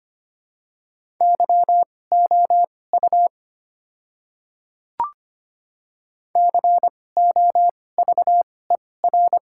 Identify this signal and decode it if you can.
Morse code, heard as a keyed tone, about 25 wpm, spaced as for 18 wpm, 700 Hz. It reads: YOU COVER
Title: 2x Character Spacing - Top 300 Words - 25wpm